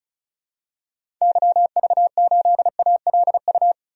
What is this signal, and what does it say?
Morse code, heard as a keyed tone, about 36 wpm, 700 Hz. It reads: YV8ALU
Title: DXCC-100 Call Signs - 6 Characters - 35wpm